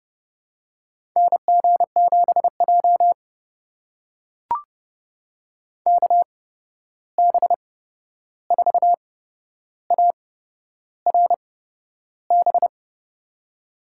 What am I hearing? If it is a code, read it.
Morse code, heard as a keyed tone, about 30 wpm, spaced as for 11 wpm, 700 Hz. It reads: NG7J KB4ARB
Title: USA Call Signs - 8x Character Spacing - 30wpm